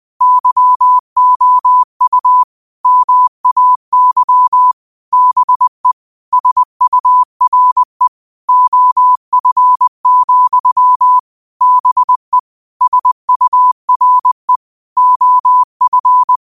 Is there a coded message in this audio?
Yes: Morse code from 0.2 to 16.4 s, heard as a keyed tone, about 20 wpm, 1000 Hz. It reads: YOU MAY BE SURE OF, BE SURE OF